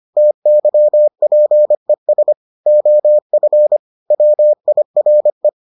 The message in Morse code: TYPES OF WIRE

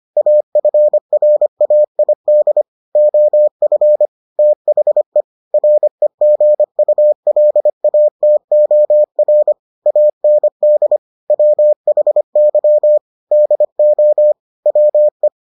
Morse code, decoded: AFRAID OF THE REGULATOR AND WHY DO WE

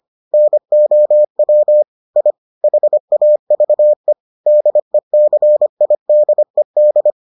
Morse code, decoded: NOW I HAVE DECIDED

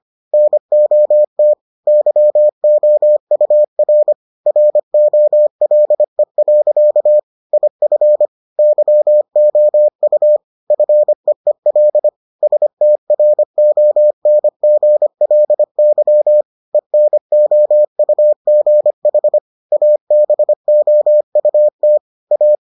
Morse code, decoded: NOT YOUR ROLE. IF YOU FEEL STRONGLY ENOUGH ABOUT A

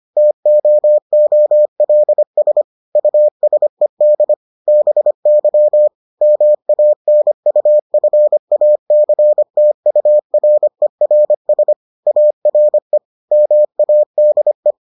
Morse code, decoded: TOOLS USED BY MANUFACTURERS ARE MADE

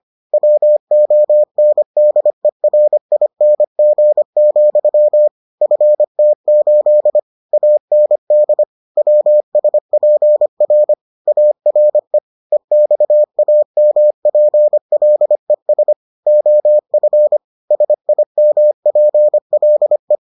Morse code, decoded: WONDERING, FT8 AND WSPR ARE EXAMPLES OF SIMPLE

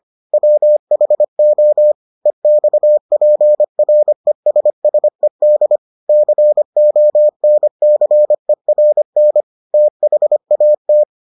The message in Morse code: WHO EXPRESSED CONCERN THAT

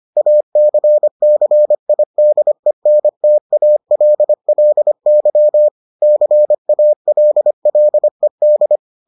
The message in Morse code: ACCIDENTALLY CALLED

